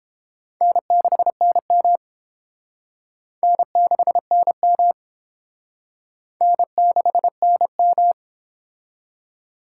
Morse code, heard as a keyed tone, about 30 wpm, 700 Hz. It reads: N6NM N6NM N6NM